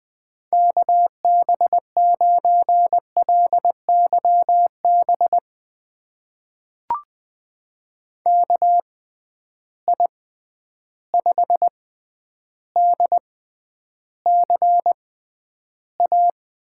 Morse code, heard as a keyed tone, about 20 wpm, spaced as for 10 wpm, 700 Hz. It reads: KB9LYB KI5DCA
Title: USA Call Signs - 6x Character Spacing - 20wpm